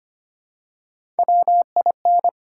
Morse code, heard as a keyed tone, about 25 wpm, 700 Hz. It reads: WIN
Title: Three Letter Words - 25wpm